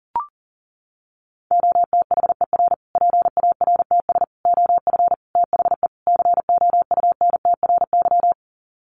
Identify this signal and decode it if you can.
Morse code, heard as a keyed tone, about 40 wpm, 700 Hz. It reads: OTHER PARTS OF THE COUNTRY
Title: Sets of 5 Words Encore 40wpm